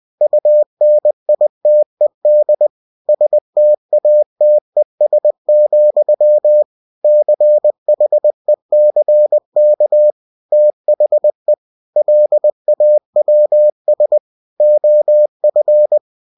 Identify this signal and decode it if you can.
Morse code, heard as a keyed tone, about 20 wpm, 600 Hz. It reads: UNITED STATES, CHECK THE LAWS OF
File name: cw-books / winnie-the-pooh / winnie-20wpm-page-0114